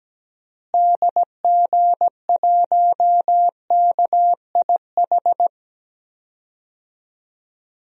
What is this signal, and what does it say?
Morse code, heard as a keyed tone, about 17 wpm, 700 Hz. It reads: DG1KIH
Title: DXCC-100 Call Signs - 6 Characters - Repeated 3x - 17wpm